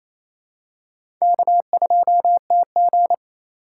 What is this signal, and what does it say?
Morse code, heard as a keyed tone, about 28 wpm, 700 Hz. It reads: K2TG
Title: USA Call Signs - 4x Character Spacing - 28wpm